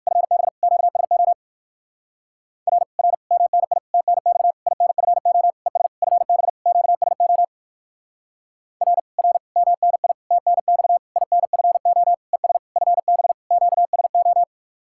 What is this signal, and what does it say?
Morse code, heard as a keyed tone, about 59 wpm, 700 Hz. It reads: FB QSO R R MNI TNX INFO ES FB QSO R R MNI TNX INFO ES FB QSO